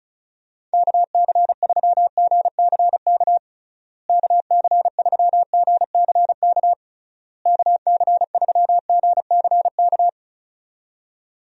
Morse code, heard as a keyed tone, about 36 wpm, 700 Hz. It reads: KC3GCK KC3GCK KC3GCK
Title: USA Call Signs - 6 Characters - Repeated 3x - 35wpm